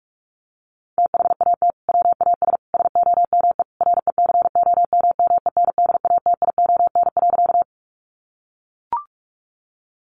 Morse code, heard as a keyed tone, about 44 wpm, 700 Hz. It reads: THAT WAS SOME RECOMMENDATION.
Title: Sentences from Top 1400 Words 45wpm